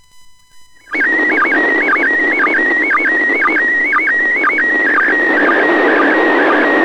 SSTV (slow-scan television), PD 120.
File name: ISS 21h24 UTC 145MHz800 pic02